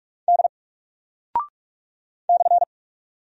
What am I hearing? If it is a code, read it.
Morse code, heard as a keyed tone, about 44 wpm, 700 Hz. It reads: D /
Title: Single Character (Letter:Number:Punctuation:Prosign) 45wpm